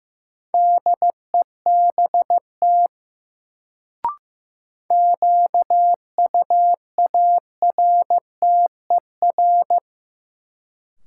Morse code, heard as a keyed tone, about 15 wpm, 700 Hz. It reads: DEBT QUARTER
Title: Sentences from Top 1300 Words - New Word Drill Alt - 15wpm